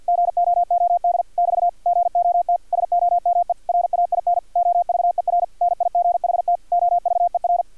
Morse code, heard as a keyed tone, about 50 wpm, 700 Hz, with radio noise. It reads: GOOD <BT> GOT SOME RAIN OVER NIGHT OVER
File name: archived / CRSnet